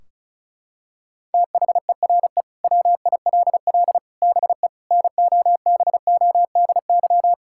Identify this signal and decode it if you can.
Morse code, heard as a keyed tone, about 36 wpm, 700 Hz. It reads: THERE WILL BE NOBODY
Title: Sentences from Top 1600 Words 35wpm